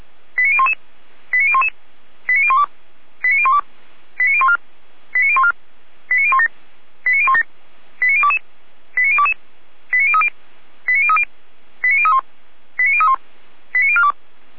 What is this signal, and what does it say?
Five-tone selective call (ZVEI), heard as quick runs of five tones, tones of about 70 ms. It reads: 89011 89011 89012 89012 89014 89014 89017 89017 89022 89022 89030 89030 89031 89031 89042